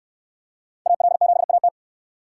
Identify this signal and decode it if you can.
Morse code, heard as a keyed tone, about 63 wpm, 700 Hz. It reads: AL7AT